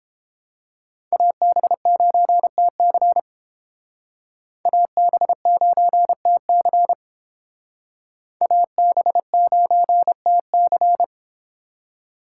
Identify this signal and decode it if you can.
Morse code, heard as a keyed tone, about 31 wpm, 700 Hz. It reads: AB9TC AB9TC AB9TC